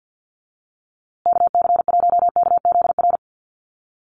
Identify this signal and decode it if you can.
Morse code, heard as a keyed tone, about 50 wpm, 700 Hz. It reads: KC1KZR
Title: USA Call Signs Repeated 2x 50wpm